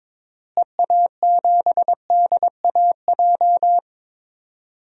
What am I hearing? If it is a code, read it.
Morse code, heard as a keyed tone, about 22 wpm, 700 Hz. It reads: EA7DAJ
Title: DXCC-100 Call Signs - 6 Characters - Repeated 2x - 22wpm